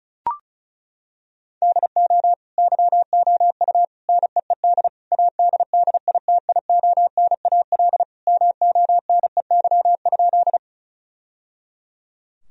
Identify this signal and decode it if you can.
Morse code, heard as a keyed tone, about 36 wpm, 700 Hz. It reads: DO YOU NEED ADDITIONAL MONEY?
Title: Sentences from Top 1000 Words - Review Part 4 - 35wpm